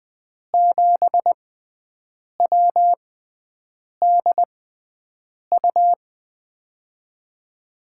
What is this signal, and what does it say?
Morse code, heard as a keyed tone, about 20 wpm, spaced as for 7 wpm, 700 Hz. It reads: 7WDU
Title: USA Call Signs - 6 Characters - 6x Character Spacing - 20wpm